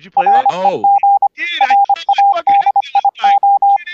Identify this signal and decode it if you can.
Morse code, heard as a keyed tone, about 25 wpm, 800 Hz, with speech and radio noise. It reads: REG RADIO